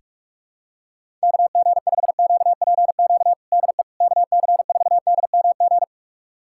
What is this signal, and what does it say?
Morse code, heard as a keyed tone, about 46 wpm, 700 Hz. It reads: KG5QPQ DE KC4DMG
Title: USA Callsign DE Callsign - 6 Characters - 45wpm